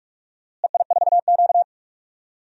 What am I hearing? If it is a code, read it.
Morse code, heard as a keyed tone, about 45 wpm, 700 Hz. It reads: EI4Q
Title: DXCC-100 Call Signs - 4 Characters - Repeated 3x - 45wpm